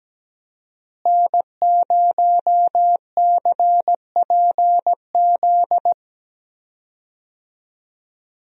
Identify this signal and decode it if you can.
Morse code, heard as a keyed tone, about 17 wpm, 700 Hz. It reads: N0CPZ
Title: USA Call Signs - 5 Characters - Repeated 3x - 17wpm